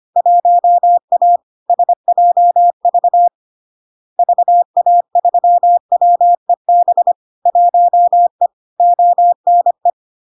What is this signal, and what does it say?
Morse code, heard as a keyed tone, about 25 wpm, 700 Hz. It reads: 1A SJV VA3WEB 1E ONE
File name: LARK-FD-2022-Log-in-CW-25wpm-700-Hz